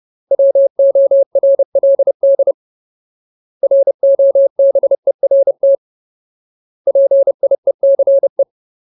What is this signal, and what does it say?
Morse code, heard as a keyed tone, about 30 wpm, 550 Hz. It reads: WORLD ROBERT PIECE
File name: Ultimate Word List / 30WPM Ultimate